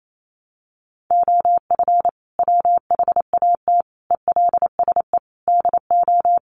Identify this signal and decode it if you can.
Morse code, heard as a keyed tone, about 28 wpm, 700 Hz. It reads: OF WHAT ELSE DO
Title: Sentences from Top 1400 Words 28wpm